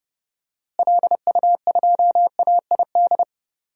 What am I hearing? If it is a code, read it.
Morse code, heard as a keyed tone, about 30 wpm, 700 Hz. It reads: LU2AID
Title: DXCC-100 Call Signs - 6 Characters - Repeated 3x - 30wpm